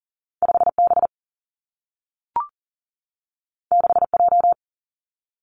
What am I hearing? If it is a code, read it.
Morse code, heard as a keyed tone, about 41 wpm, 700 Hz. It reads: 5B 6J
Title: Two Challenging Letter-Number Combinations 40wpm